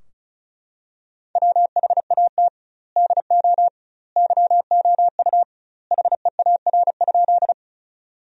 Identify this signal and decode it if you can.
Morse code, heard as a keyed tone, about 36 wpm, 700 Hz. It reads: WHAT DO YOU HEAR?
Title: 2x Word-Spacing - Sentences from Top 300 Words - 35wpm